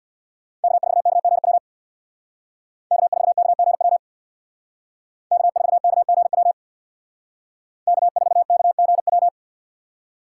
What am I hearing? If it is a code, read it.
Morse code, heard as a keyed tone, about 64 wpm, 700 Hz. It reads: K4KGW K4KGW K4KGW K4KGW